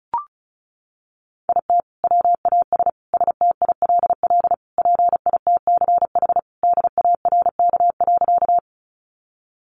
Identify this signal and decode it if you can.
Morse code, heard as a keyed tone, about 34 wpm, 700 Hz. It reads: IT WAS STILL PITCH DARK.
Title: Sentences from Top 1300 Words 35wpm